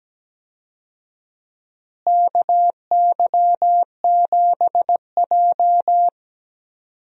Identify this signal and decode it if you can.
Morse code, heard as a keyed tone, about 17 wpm, 700 Hz. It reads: KY7J
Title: USA Call Signs - 4 Characters - Repeated 3x - 17wpm